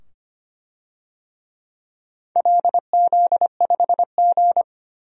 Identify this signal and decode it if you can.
Morse code, heard as a keyed tone, about 25 wpm, 700 Hz. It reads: LZ5G
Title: DXCC-100 Call Signs - 4 Characters - 25wpm